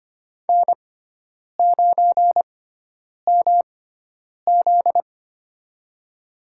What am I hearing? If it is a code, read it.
Morse code, heard as a keyed tone, about 25 wpm, spaced as for 9 wpm, 700 Hz. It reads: N9MZ
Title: USA Call Signs - 4 Characters - 6x Character Spacing - 25wpm